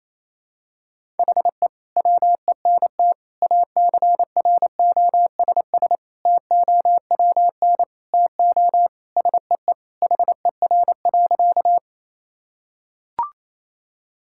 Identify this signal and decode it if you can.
Morse code, heard as a keyed tone, about 28 wpm, 700 Hz. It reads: HE WENT ACROSS TOWN TO SEE HER.
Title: Sentences from Top 1000 Words - Review Part 5 - 28wpm